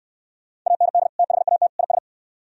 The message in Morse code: AND THAT IS

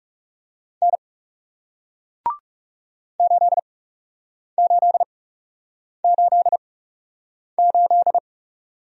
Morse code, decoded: N 8888

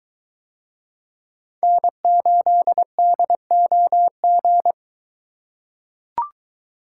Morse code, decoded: N8DOG

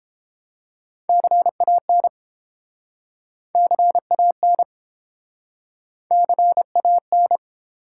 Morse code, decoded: CAN CAN CAN

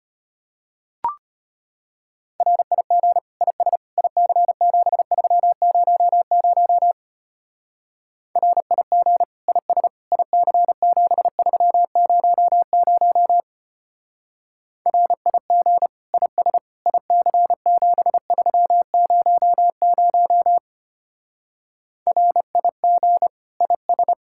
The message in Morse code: RIG IS IC7300 RIG IS IC7300 RIG IS IC7300 RIG IS

